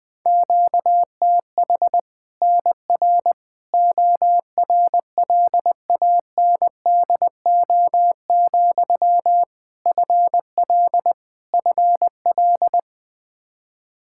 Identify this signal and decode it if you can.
Morse code, heard as a keyed tone, about 20 wpm, 700 Hz. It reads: QTH NR ORLANDO, FL FL
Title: Practice QSO - Exchange 1 - Beginning - 20wpm